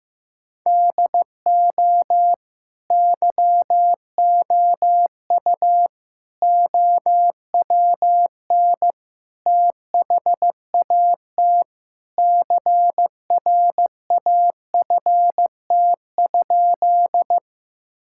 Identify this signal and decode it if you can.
Morse code, heard as a keyed tone, about 16 wpm, 700 Hz. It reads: DO YOU OWN THAT CRAFT?
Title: Sentences from Top 1000 Words - Review Part 6 - 15wpm